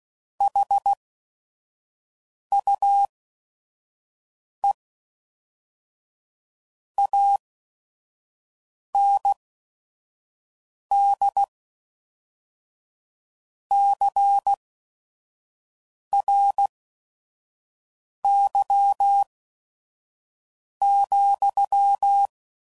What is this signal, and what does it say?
Morse code, heard as a keyed tone, about 16 wpm, spaced as for 5 wpm, 800 Hz. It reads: HUE AND CRY,